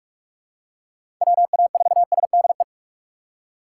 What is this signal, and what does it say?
Morse code, heard as a keyed tone, about 44 wpm, 700 Hz. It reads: WA4SDE
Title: USA Call Signs - 6 Characters - 3x Character Spacing - 45wpm